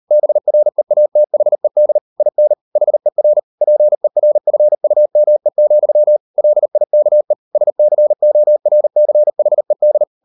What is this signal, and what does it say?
Morse code, heard as a keyed tone, about 38 wpm, 600 Hz. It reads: BREATHED IN HER PERFUME, LIKE SCORCHED